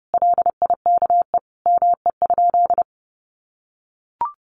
Morse code, heard as a keyed tone, about 30 wpm, 700 Hz. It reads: LIKE ME?
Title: Sentences from Top 1200 Words 30wpm